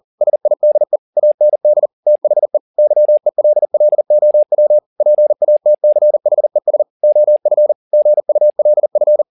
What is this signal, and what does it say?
Morse code, heard as a keyed tone, about 40 wpm, 600 Hz. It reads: SIDE AND THE YELLOW PATCHES OF GULF
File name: the old man and the sea 2 40wpm